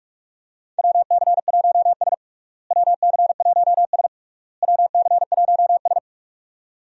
Morse code, heard as a keyed tone, about 46 wpm, 700 Hz. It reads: WC1S WC1S WC1S
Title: USA Call Signs - 4 Characters - Repeated 3x - 45wpm